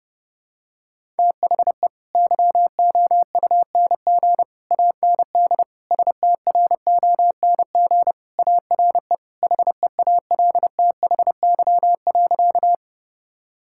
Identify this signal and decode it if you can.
Morse code, heard as a keyed tone, about 30 wpm, 700 Hz. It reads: THE YOUNG AND STRONG ARE HEALTHY.
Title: Sentences from Top 1000 Words - Review Part 5 - 30wpm